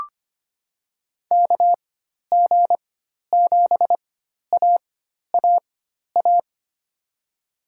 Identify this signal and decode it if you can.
Morse code, heard as a keyed tone, about 25 wpm, spaced as for 12 wpm, 700 Hz. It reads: KG7AAA
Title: USA Call Signs - 6 Characters - 4x Character Spacing - 25wpm